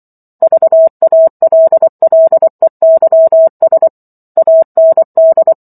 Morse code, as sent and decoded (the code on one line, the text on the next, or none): ...- .- .-.. .-.. -.-- ... / .- -. -..
VALLEYS AND